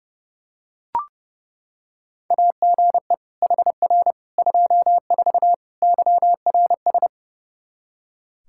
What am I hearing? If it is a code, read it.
Morse code, heard as a keyed tone, about 29 wpm, 700 Hz. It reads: AGE HR 24 YRS
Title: Practice QSO - Exchange 3 - More Info - 30wpm